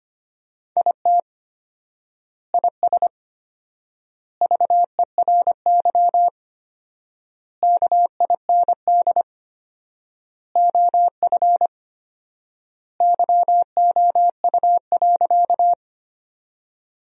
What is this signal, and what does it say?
Morse code, heard as a keyed tone, about 25 wpm, 700 Hz. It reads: IT IS VERY KIND OF YOU.
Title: 4x Word-Spacing - Sentences from Top 200 Words - 25wpm